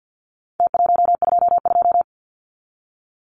Morse code, heard as a keyed tone, about 51 wpm, 700 Hz. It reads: T122
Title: Numbers - Contest Style - 4-Digit 50wpm